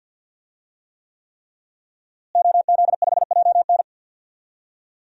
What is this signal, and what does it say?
Morse code, heard as a keyed tone, about 49 wpm, 700 Hz. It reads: OZ5JN
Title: DXCC-100 Call Signs - 5 Characters - Repeated 3x - 50wpm